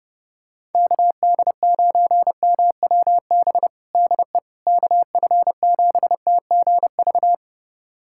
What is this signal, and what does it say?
Morse code, heard as a keyed tone, about 30 wpm, 700 Hz. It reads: KD9MWB DE KF7TGV
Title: USA Callsign DE Callsign - 6 Characters - 30wpm